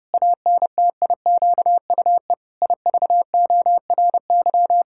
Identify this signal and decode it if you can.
Morse code, heard as a keyed tone, about 30 wpm, 700 Hz. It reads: ANTIQUE IVORY
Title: Pangram Challenge v2 - 30wpm